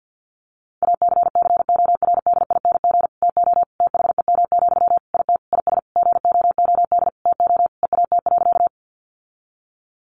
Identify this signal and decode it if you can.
Morse code, heard as a keyed tone, about 50 wpm, 700 Hz. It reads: ACCORDING TO THEM, IT IS GOOD TO EAT.
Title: Sentences from Top 800 Words 50wpm